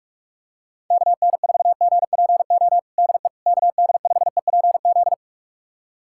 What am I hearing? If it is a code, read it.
Morse code, heard as a keyed tone, about 46 wpm, 700 Hz. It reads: KN4GPO DE KD5EPZ
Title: USA Callsign DE Callsign - 6 Characters - 45wpm